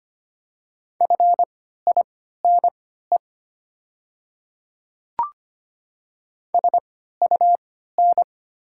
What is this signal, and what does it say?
Morse code, heard as a keyed tone, about 25 wpm, spaced as for 14 wpm, 700 Hz. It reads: FINE SUN
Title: 3x Character Spacing - Top 500 Words - 25wpm